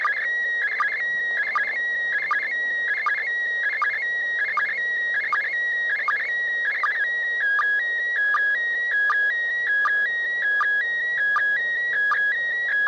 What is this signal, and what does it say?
SSTV (slow-scan television), PD 180.
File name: SSTV / ISS2015AprilPD180x